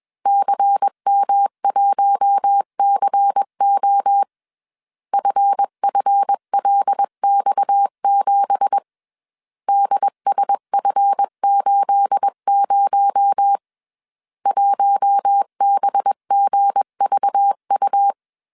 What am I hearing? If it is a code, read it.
Morse code, heard as a keyed tone, about 21 wpm, 800 Hz. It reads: CM1CO FFLX7 DSF80 1BGVU